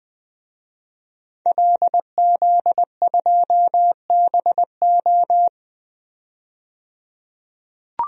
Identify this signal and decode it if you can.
Morse code, heard as a keyed tone, about 20 wpm, 700 Hz. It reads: LZ2BO
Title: DXCC-100 Call Signs - 5 Characters - 20wpm